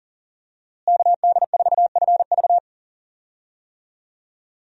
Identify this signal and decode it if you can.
Morse code, heard as a keyed tone, about 40 wpm, 700 Hz. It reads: KD4FV
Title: USA Call Signs - 5 Characters - Repeated 2x - 40wpm